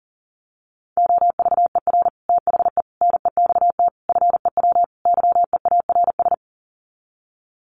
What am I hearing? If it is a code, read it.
Morse code, heard as a keyed tone, about 40 wpm, 700 Hz. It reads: OVER THE NEXT FEW YEARS
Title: Sets of 5 Words Encore 40wpm